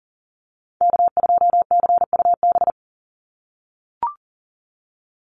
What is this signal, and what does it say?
Morse code, heard as a keyed tone, about 40 wpm, 700 Hz. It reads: K2CUB
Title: Call Signs - 40wpm